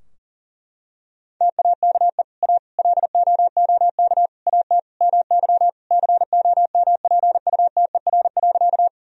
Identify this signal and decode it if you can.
Morse code, heard as a keyed tone, about 40 wpm, 700 Hz. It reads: TAKE A LOOK AT MY COMPUTER.
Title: Sentences from Top 1000 Words - Review Part 3 - 40wpm